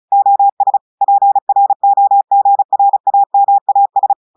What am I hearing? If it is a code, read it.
Morse code, heard as a keyed tone, about 35 wpm, 800 Hz. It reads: OS PROGRAMAS
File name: audios / portugais / portugais texte en clair 2 vitesse 35 mots minutes